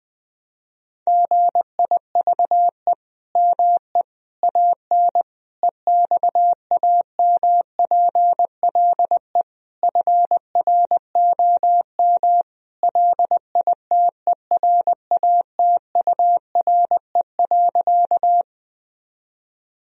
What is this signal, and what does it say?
Morse code, heard as a keyed tone, about 20 wpm, 700 Hz. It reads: GIVE ME AN EXAMPLE FROM LITERATURE.
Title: Sentences from Top 1000 Words - Review Part 3 - 20wpm